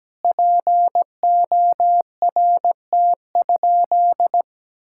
Morse code, heard as a keyed tone, about 17 wpm, 700 Hz. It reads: PORT?
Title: Sentences from Top 1000 Words - Review Part 5 - 17wpm